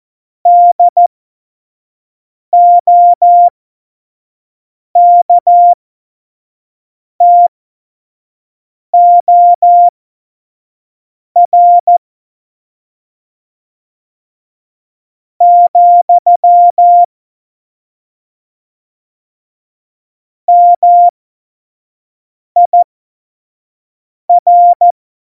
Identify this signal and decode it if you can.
Morse code, heard as a keyed tone, about 14 wpm, spaced as for 4 wpm, 700 Hz. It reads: DOKTOR, MIR